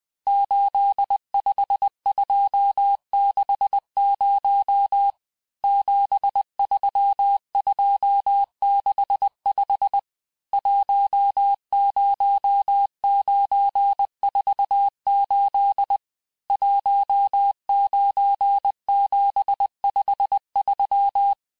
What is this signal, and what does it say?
Morse code, heard as a keyed tone, about 20 wpm, 800 Hz. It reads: 85260 73265 10948 19753
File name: cours-1584-teste-chiffre-vitesse-19mpm